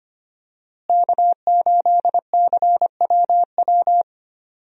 Morse code, heard as a keyed tone, about 25 wpm, 700 Hz. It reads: K8CWW
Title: USA Call Signs - 5 Characters - Repeated 2x - 25wpm